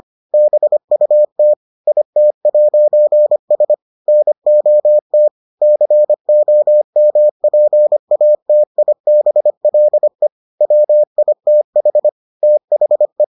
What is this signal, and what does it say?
Morse code, heard as a keyed tone, about 25 wpm, 600 Hz. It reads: BUT IT'S NOT COMPATIBLE WITH THE